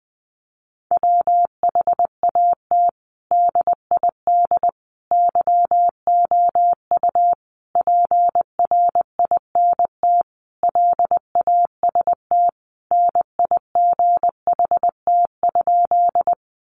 Morse code, heard as a keyed tone, about 20 wpm, 700 Hz. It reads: WHAT DID YOU PRINT LAST NIGHT?
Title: Sentences from Top 1400 Words 20wpm